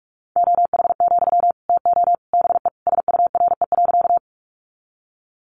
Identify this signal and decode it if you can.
Morse code, heard as a keyed tone, about 44 wpm, 700 Hz. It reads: OH, TO BE SURE.
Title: Sentences from Top 500 Words 45wpm